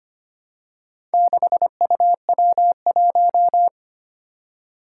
Morse code, heard as a keyed tone, about 25 wpm, 700 Hz. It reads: 6UW1
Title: Four Challenging Letter-Number Combinations 25wpm